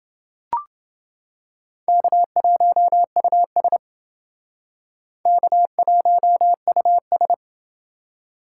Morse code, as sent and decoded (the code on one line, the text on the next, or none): -.- .---- ..- ... / -.- .---- ..- ...
K1US K1US